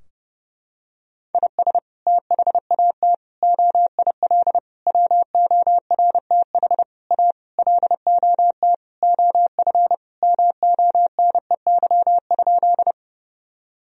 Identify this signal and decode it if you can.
Morse code, heard as a keyed tone, about 29 wpm, 700 Hz. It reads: IS THAT OIL WORTH A LOT OF MONEY?
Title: Sentences from Top 1000 Words - Review Part 4 - 30wpm